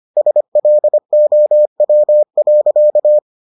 Morse code, heard as a keyed tone, about 25 wpm, 600 Hz. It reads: SLOW.